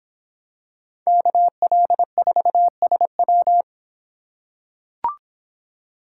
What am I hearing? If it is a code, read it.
Morse code, heard as a keyed tone, about 26 wpm, 700 Hz. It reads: KL4SW